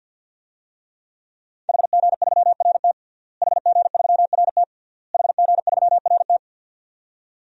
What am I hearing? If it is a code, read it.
Morse code, heard as a keyed tone, about 52 wpm, 700 Hz. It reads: HG3RT HG3RT HG3RT
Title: DXCC-100 Call Signs - 5 Characters - Repeated 3x - 50wpm